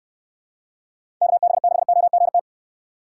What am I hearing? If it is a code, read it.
Morse code, heard as a keyed tone, about 66 wpm, 700 Hz. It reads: KB6OGT